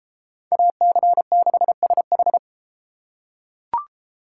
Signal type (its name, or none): Morse code